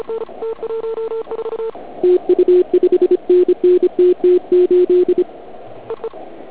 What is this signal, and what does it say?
Morse code, heard as a keyed tone, about 26 wpm, 360 Hz. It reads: TU 5NN TT8
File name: Archiv / CUC106 / cuc106 PILEUP